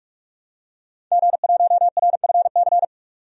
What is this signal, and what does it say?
Morse code, heard as a keyed tone, about 46 wpm, 700 Hz. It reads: G1RFC
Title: DXCC-100 Call Signs - 5 Characters - 45wpm